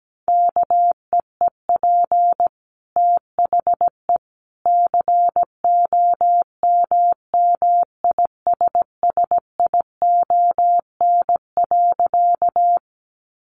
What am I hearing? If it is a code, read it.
Morse code, heard as a keyed tone, about 17 wpm, 700 Hz. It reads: KEEP THE COMMISSION.
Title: Sentences from Top 1400 Words 17wpm